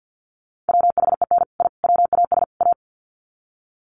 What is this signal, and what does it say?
Morse code, heard as a keyed tone, about 50 wpm, 700 Hz. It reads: WHEN I WAS A
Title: Sets of 4 Words 50wpm